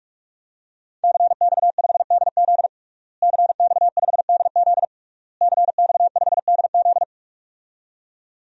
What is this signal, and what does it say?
Morse code, heard as a keyed tone, about 46 wpm, 700 Hz. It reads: CX5DZ CX5DZ CX5DZ
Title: DXCC-100 Call Signs - 5 Characters - Repeated 3x - 45wpm